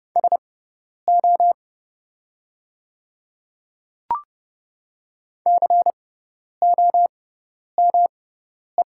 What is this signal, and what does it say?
Morse code, heard as a keyed tone, about 29 wpm, spaced as for 10 wpm, 700 Hz. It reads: SO COME
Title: 6x Character Spacing - Top 500 Words - 30wpm